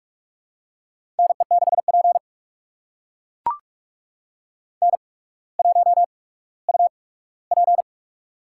Morse code, heard as a keyed tone, about 44 wpm, spaced as for 19 wpm, 700 Hz. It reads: NE6P N1UP
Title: USA Call Signs - 8x Character Spacing - 45wpm